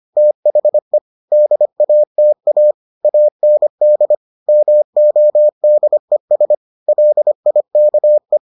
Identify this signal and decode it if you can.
Morse code, heard as a keyed tone, about 25 wpm, 600 Hz. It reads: THE DATA AND MODES LIKE